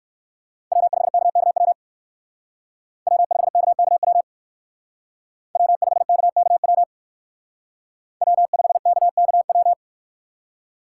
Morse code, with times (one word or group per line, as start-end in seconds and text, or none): 0.7-1.7 W5KKW
3.1-4.2 W5KKW
5.6-6.8 W5KKW
8.2-9.7 W5KKW